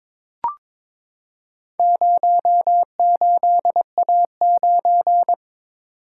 Morse code, heard as a keyed tone, about 22 wpm, 700 Hz. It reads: 08A9
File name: practice / 22wpm / Four Letter-Number Combinations - 22wpm